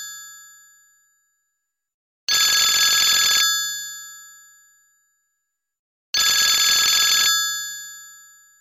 Emergency Alert System (SAME) tones (header bursts), heard as harsh buzzing bursts. It is heard from 1.1 to 8.3 s.